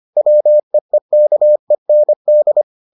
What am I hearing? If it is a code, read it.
Morse code, heard as a keyed tone, about 25 wpm, 600 Hz. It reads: WEEKEND